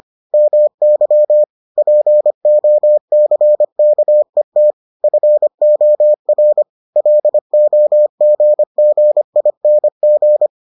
Morse code, heard as a keyed tone, about 25 wpm, 600 Hz. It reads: MY POCKET FOR LOGGING